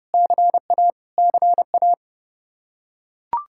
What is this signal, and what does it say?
Morse code, heard as a keyed tone, about 29 wpm, 700 Hz. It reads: CA CA
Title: Practice QSO - Exchange 1 - Beginning - 30wpm